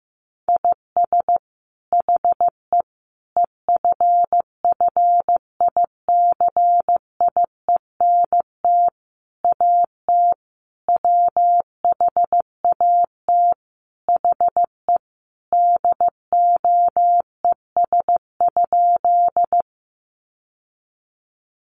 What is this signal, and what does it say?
Morse code, heard as a keyed tone, about 15 wpm, 700 Hz. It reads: IS HE EFFICIENT AT WHAT HE DOES?
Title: Sentences from Top 1300 Words 15wpm